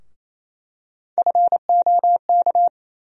Morse code, heard as a keyed tone, about 28 wpm, 700 Hz. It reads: FOK
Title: USA Call Sign Postfixes - 3 Letters 28wpm